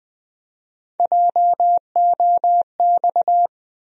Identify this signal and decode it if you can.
Morse code, heard as a keyed tone, about 20 wpm, 700 Hz. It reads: JOX
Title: USA Call Sign Postfixes - 3 Letters 20wpm